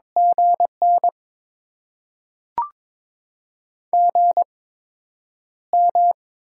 Morse code, heard as a keyed tone, about 23 wpm, spaced as for 9 wpm, 700 Hz. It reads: GN GM
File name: Top 20 QSO Elements - 8x Character Spacing - 22wpm